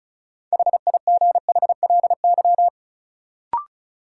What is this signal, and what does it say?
Morse code, heard as a keyed tone, about 34 wpm, 700 Hz. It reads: HIGHLY
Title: Sentences from Top 1100 Words - New Word Drill Alt - 35wpm